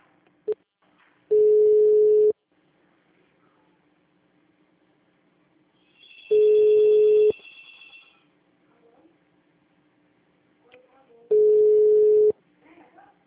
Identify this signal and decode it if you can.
Ringback tone (Europe, 425 Hz), with speech.
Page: CallCenter 2